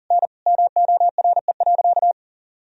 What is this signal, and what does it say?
Morse code, heard as a keyed tone, about 40 wpm, 700 Hz. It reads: N MORE.
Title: Sentences from Top 300 Words and Abbreviations 40wpm